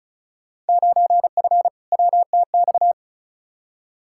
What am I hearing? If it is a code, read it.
Morse code, heard as a keyed tone, about 34 wpm, 700 Hz. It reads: 9F WTX
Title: Field Day Exchanges Encore - 35wpm